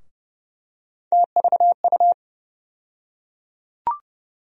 Morse code, heard as a keyed tone, about 32 wpm, 700 Hz. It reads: TVU
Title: USA Call Sign Postfixes - 3 Letters 30wpm